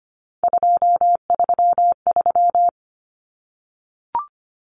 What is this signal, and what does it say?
Morse code, heard as a keyed tone, about 25 wpm, 700 Hz. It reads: 233
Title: Numbers - 3-Digit 25wpm